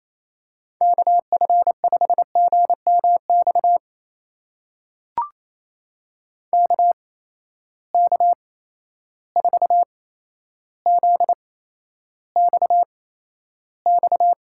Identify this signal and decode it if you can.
Morse code, heard as a keyed tone, about 28 wpm, spaced as for 12 wpm, 700 Hz. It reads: KF5GMX KK4ZXX
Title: USA Call Signs - 6 Characters - 8x Character Spacing - 28wpm